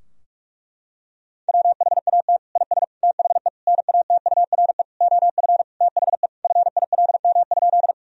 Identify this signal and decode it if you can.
Morse code, heard as a keyed tone, about 41 wpm, 700 Hz. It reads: WHAT IS THE NATURE OF THE FILM?
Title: Sentences from Top 1000 Words - Review Part 3 - 45wpm